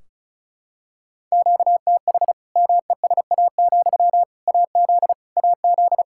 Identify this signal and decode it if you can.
Morse code, heard as a keyed tone, about 36 wpm, 700 Hz. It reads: QTH MESA, AZ AZ
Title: Practice QSO - Exchange 1 - Beginning - 35wpm